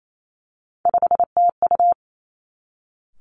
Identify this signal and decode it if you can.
Morse code, heard as a keyed tone, about 28 wpm, 700 Hz. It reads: 5TU
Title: Three Challenging Letter-Number Combinations 28wpm